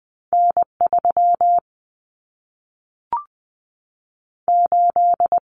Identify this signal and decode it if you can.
Morse code, heard as a keyed tone, about 20 wpm, 700 Hz. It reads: N3 8
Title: Common Cut Numbers - 2-Digit 20wpm